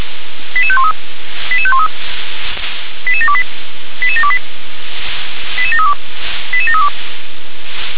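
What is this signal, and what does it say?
Five-tone selective call (ZVEI), heard as quick runs of five tones, tones of about 70 ms. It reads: 88513 88513 88728 88728 88732 88732
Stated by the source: ZVEI five-tone selective call received by SDR.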